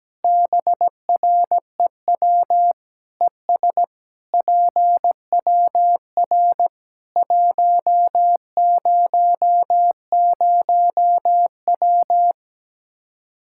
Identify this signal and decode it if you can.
Morse code, heard as a keyed tone, about 17 wpm, 700 Hz. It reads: BREW ES PWR 100W
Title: Practice QSO - Exchange 2 - Info - 17wpm